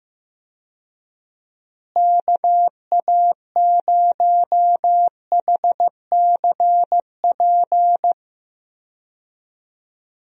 Morse code, heard as a keyed tone, about 15 wpm, 700 Hz. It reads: KA0HCP
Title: USA Call Signs - 6 Characters - Repeated 3x - 15wpm